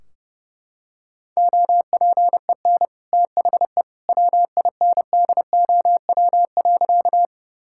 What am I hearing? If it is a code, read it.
Morse code, heard as a keyed tone, about 32 wpm, 700 Hz. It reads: OPEN THE WINDOW.
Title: Sentences from Top 1600 Words 30wpm